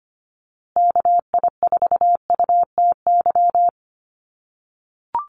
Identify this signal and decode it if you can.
Morse code, heard as a keyed tone, about 25 wpm, 700 Hz. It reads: KI4UTY
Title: USA Call Signs Repeated 2x 25wpm